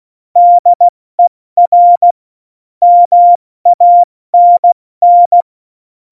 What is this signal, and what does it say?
Morse code, heard as a keyed tone, about 16 wpm, spaced as for 14 wpm, 700 Hz. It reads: DER MANN
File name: Morsekurs / Witze / Witz 29. Oktober 2019, 70 F80